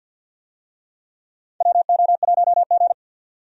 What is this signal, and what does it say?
Morse code, heard as a keyed tone, about 48 wpm, 700 Hz. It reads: WO1G
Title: USA Call Signs - 4 Characters - 50wpm